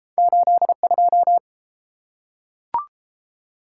Morse code, heard as a keyed tone, about 30 wpm, 700 Hz. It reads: 8 2